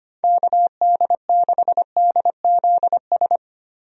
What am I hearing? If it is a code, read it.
Morse code, heard as a keyed tone, about 25 wpm, 700 Hz. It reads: KD6DZS
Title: USA Call Signs - 6 Characters - 25wpm